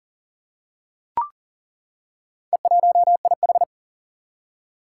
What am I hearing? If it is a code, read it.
Morse code, heard as a keyed tone, about 40 wpm, 700 Hz. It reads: E1IH
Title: Four Challenging Letter-Number Combinations 40wpm